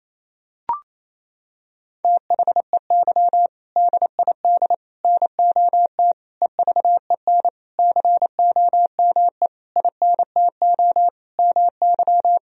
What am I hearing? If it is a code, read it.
Morse code, heard as a keyed tone, about 28 wpm, 700 Hz. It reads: THEY DID NOT EVEN COME INTO MY
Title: Sentences from Top 1000 Words - Review Part 1 - 28wpm